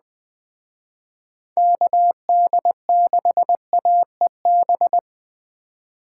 Morse code, heard as a keyed tone, about 20 wpm, 700 Hz. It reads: KD6AEB